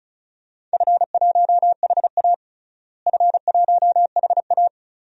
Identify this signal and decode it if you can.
Morse code, heard as a keyed tone, about 36 wpm, 700 Hz. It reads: F1HA F1HA
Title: DXCC-100 Call Signs - 4 Characters - Repeated 2x - 35wpm